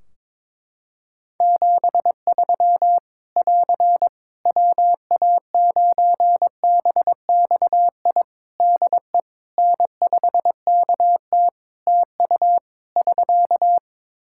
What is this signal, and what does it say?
Morse code, heard as a keyed tone, about 22 wpm, 700 Hz. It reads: 73 <AR> WA9BXI DE N5KT TU <SK>
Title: Practice QSO - Exchange 4 - Ending - 22wpm